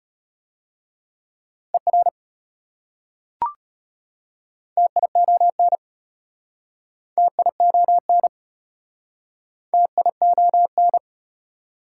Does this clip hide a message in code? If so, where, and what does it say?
Yes: Morse code from 1.7 to 11.0 s, heard as a keyed tone, about 33 wpm, 700 Hz. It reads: ER TION TION TION